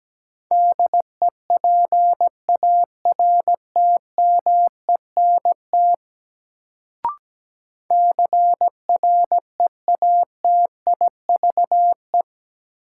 Morse code, heard as a keyed tone, about 17 wpm, 700 Hz. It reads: DEPARTMENT CREATIVE
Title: Sentences from Top 1000 Words - Review Part 5 - New Word Drill - 17wpm